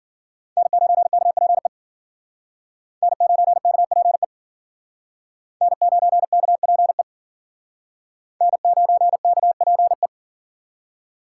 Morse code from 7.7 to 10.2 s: N9KPE

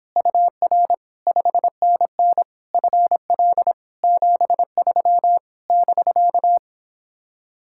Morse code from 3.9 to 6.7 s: 73 <BK>

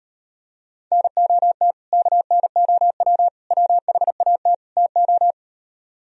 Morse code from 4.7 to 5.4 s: TO

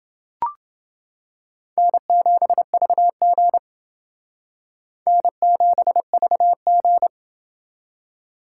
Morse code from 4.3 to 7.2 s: N7VG

